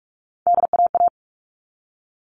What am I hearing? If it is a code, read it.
Morse code, heard as a keyed tone, about 44 wpm, 700 Hz. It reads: DAA
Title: All Cut Numbers - 3-Digit 45wpm